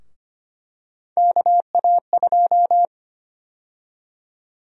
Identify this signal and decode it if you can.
Morse code, heard as a keyed tone, about 25 wpm, 700 Hz. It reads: KA2
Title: USA Call Sign Prefixes - 25wpm